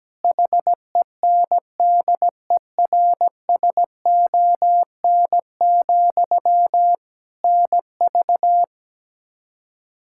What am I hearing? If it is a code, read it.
Morse code, heard as a keyed tone, about 17 wpm, 700 Hz. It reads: HENDERSON, NV
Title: Practice QSO - Exchange 1 - Beginning - 17wpm